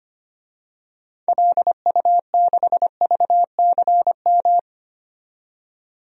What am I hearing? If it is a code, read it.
Morse code, heard as a keyed tone, about 25 wpm, 700 Hz. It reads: LU6VCM
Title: DXCC-100 Call Signs - 6 Characters - 25wpm